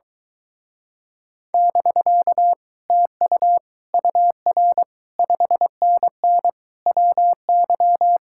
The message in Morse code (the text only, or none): <BK> TU UR 5NN WY